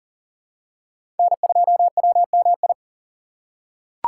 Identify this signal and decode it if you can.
Morse code, heard as a keyed tone, about 39 wpm, 700 Hz. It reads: N2WMI
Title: USA Call Signs - 4x Character Spacing - 40wpm